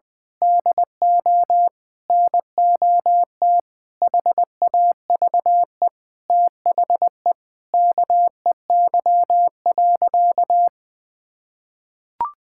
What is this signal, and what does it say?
Morse code, heard as a keyed tone, about 20 wpm, 700 Hz. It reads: DO NOT HAVE THE KEY.
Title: Sentences with 5-Word Phrases - Spoken Beginning - 20wpm